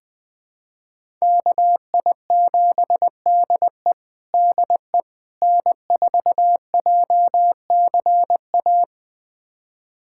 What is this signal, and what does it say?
Morse code, heard as a keyed tone, about 21 wpm, 700 Hz. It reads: KI7DE DE N4JCA
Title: USA Callsign DE Callsign - 5 Characters - 20wpm